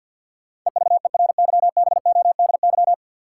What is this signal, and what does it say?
Morse code, heard as a keyed tone, about 52 wpm, 700 Hz. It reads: EVERYBODY
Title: Sentences from Top 1400 Words - New Word Drill - 50wpm